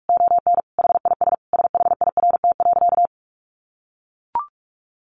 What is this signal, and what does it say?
Morse code, heard as a keyed tone, about 44 wpm, 700 Hz. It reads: ON HIS SHIRT.
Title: Additional Easy Phrases 45wpm